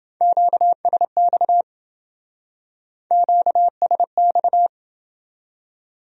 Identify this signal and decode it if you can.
Morse code, heard as a keyed tone, about 28 wpm, 700 Hz. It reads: QSX QSX